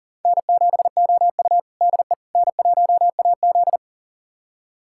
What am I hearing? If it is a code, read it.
Morse code, heard as a keyed tone, about 40 wpm, 700 Hz. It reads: N7OU DE N1AZ
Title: USA Callsign DE Callsign - 4 Characters - 40wpm